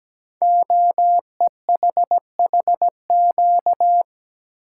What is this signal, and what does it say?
Morse code, heard as a keyed tone, about 17 wpm, 700 Hz. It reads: OEHHQ
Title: Code Groups - 5 Letters - 17wpm